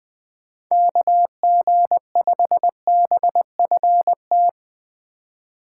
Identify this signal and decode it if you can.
Morse code, heard as a keyed tone, about 20 wpm, 700 Hz. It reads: KG5BFT